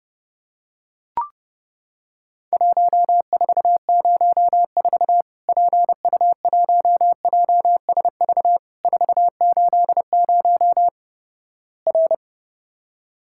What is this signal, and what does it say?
Morse code, heard as a keyed tone, about 31 wpm, 700 Hz. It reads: 1404 PU1JSV 480 R
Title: Worked All Europe DX Contest - QTC Exchange - 30wpm